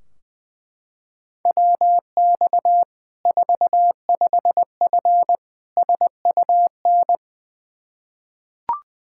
Morse code, heard as a keyed tone, about 20 wpm, 700 Hz. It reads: WX 45F SUN
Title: Practice QSO - Exchange 2 - Info - 20wpm